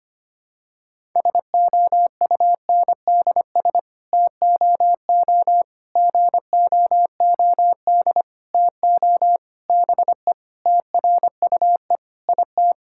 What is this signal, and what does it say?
Morse code, heard as a keyed tone, about 25 wpm, 700 Hz. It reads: SOUNDS TOO GOOD TO BE TRUE IT
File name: Sets of 9 Words - 25wpm